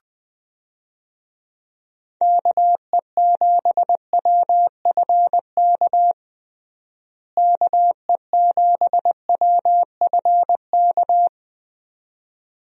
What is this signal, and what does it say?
Morse code, heard as a keyed tone, about 20 wpm, 700 Hz. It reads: KE7WFK KE7WFK
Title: USA Call Signs - 6 Characters - Repeated 2x - 20wpm